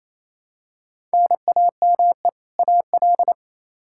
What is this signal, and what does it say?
Morse code, heard as a keyed tone, about 28 wpm, 700 Hz. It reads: NAME AL
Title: K1USN SST QSO Practice 28wpm